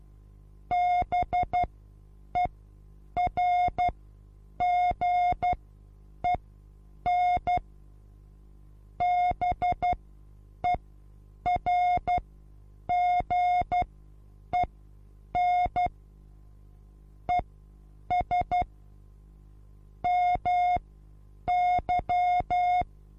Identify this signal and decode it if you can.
Morse code, heard as a keyed tone, about 12 wpm, spaced as for 8 wpm, 720 Hz. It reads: BERGEN BERGEN ES MY
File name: Onlinelehrgang / morsen / DJ4UF Morsen23